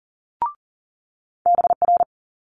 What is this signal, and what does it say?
Morse code, heard as a keyed tone, about 43 wpm, 700 Hz. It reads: BR
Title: Multi-letter Phonemes Quick Fire 40wpm